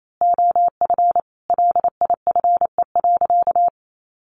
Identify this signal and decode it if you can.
Morse code, heard as a keyed tone, about 28 wpm, 700 Hz. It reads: OF LIFE.
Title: A Sound Of Thunder 28wpm